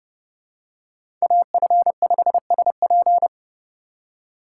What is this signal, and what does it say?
Morse code, heard as a keyed tone, about 28 wpm, 700 Hz. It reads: AF5SP